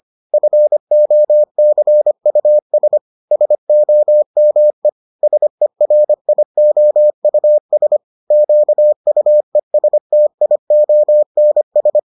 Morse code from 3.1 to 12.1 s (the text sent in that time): SOME SERIOUS QUESTIONS